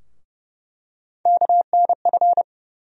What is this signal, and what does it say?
Morse code, heard as a keyed tone, about 30 wpm, 700 Hz. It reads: KNF
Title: USA Call Sign Postfixes - 3 Letters 30wpm